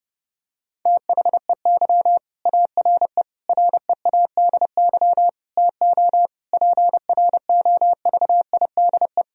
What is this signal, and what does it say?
Morse code, heard as a keyed tone, about 30 wpm, 700 Hz. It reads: THEY ARE READY TO PROVIDE
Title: Sentences from Top 1000 Words - Review Part 3 - 30wpm